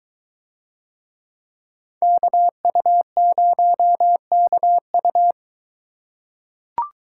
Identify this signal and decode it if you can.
Morse code, heard as a keyed tone, about 23 wpm, 700 Hz. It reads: KU0KU